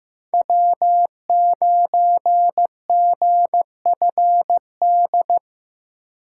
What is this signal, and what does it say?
Morse code, heard as a keyed tone, about 15 wpm, 700 Hz. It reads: W9GFD
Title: USA Call Signs - 5 Characters - 2x Character Spacing - 15wpm